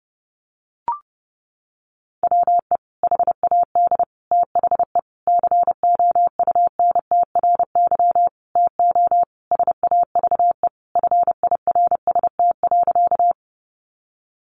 Morse code, heard as a keyed tone, about 30 wpm, 700 Hz. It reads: WE HAD THE COUNTRY TO SAVE FIRST.
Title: Sentences from Top 1100 Words 30wpm